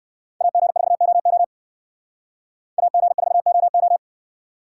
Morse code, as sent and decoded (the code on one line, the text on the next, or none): .- --. ....- --- --- / .- --. ....- --- ---
AG4OO AG4OO